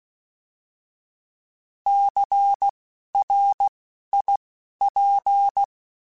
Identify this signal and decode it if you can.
Morse code, heard as a keyed tone, about 16 wpm, spaced as for 12 wpm, 800 Hz. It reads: CRIP
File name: Alphabetical / Alphabetical mp3s / Cs @10